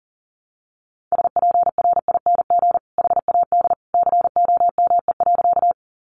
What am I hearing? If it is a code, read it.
Morse code, heard as a keyed tone, about 40 wpm, 700 Hz. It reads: SPRING HAD COME.
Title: Sentences from Top 1200 Words 40wpm